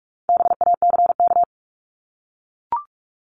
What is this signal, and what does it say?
Morse code, heard as a keyed tone, about 44 wpm, 700 Hz. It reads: BACK
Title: Top 300 Words Repeated 45wpm